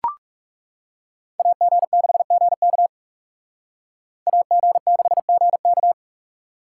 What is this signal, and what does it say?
Morse code, heard as a keyed tone, about 34 wpm, 700 Hz. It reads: AG6GK AG6GK
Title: Speed Racing - USA 2x2 Call Signs 30wpm